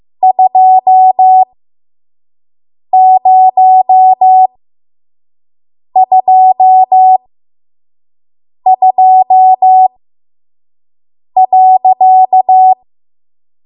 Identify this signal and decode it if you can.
Morse code, heard as a keyed tone, about 15 wpm, spaced as for 5 wpm, 750 Hz. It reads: 2022.
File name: Morse / 05WPM